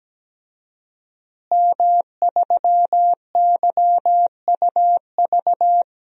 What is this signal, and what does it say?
Morse code, heard as a keyed tone, about 17 wpm, 700 Hz. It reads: M3YUV
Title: DXCC-100 Call Signs - 5 Characters - Repeated 2x - 17wpm